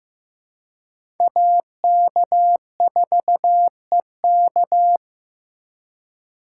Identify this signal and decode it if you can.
Morse code, heard as a keyed tone, about 15 wpm, 700 Hz. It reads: AK4EK
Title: USA Call Signs - 5 Characters - 2x Character Spacing - 15wpm